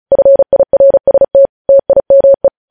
Morse code, heard as a keyed tone, about 36 wpm, 550 Hz. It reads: FIRST TIME
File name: podcast / News headlines-1x-35x35